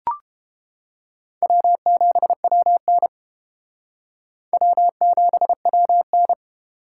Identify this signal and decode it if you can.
Morse code, heard as a keyed tone, about 31 wpm, 700 Hz. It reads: W7WN W7WN